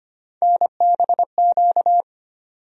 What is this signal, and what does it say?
Morse code, heard as a keyed tone, about 25 wpm, 700 Hz. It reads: NBQ
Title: USA Call Sign Postfixes - 3 Letters 25wpm